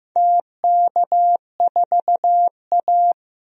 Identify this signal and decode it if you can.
Morse code, heard as a keyed tone, about 15 wpm, 700 Hz. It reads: TK4A
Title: DXCC-100 Call Signs - 4 Characters - Repeated 2x - 15wpm